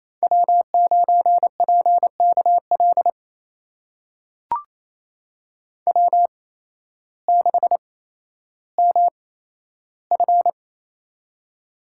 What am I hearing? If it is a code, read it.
Morse code, heard as a keyed tone, about 28 wpm, spaced as for 13 wpm, 700 Hz. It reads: W9PKL W6MF
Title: USA Call Signs - 8x Character Spacing - 28wpm